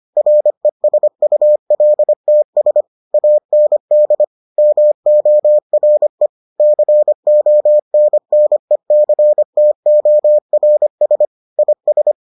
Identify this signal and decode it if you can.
Morse code, heard as a keyed tone, about 25 wpm, 600 Hz. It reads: RESULTS AND MORE CONNECTORS IS